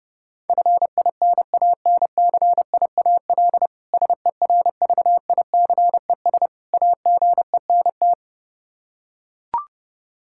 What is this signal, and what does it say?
Morse code, heard as a keyed tone, about 30 wpm, 700 Hz. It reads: FINANCIAL SERVICES AGENT
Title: Practice QSO - Exchange 3 - More Info - 30wpm